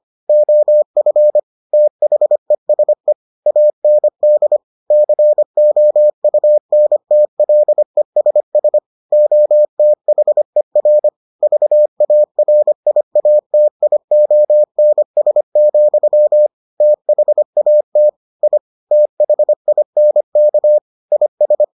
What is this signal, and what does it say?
Morse code, heard as a keyed tone, about 25 wpm, 600 Hz. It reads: OF THESE AND COUNTLESS OTHER VARIATIONS, THAT I THINK IS